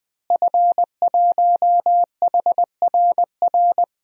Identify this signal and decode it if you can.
Morse code, heard as a keyed tone, about 21 wpm, 700 Hz. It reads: F1HRR
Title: DXCC-100 Call Signs - 5 Characters - Repeated 3x - 20wpm